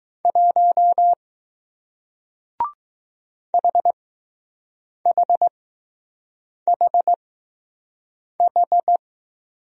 Morse code, heard as a keyed tone, about 20 wpm, spaced as for 6 wpm, 700 Hz. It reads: 1 HHHH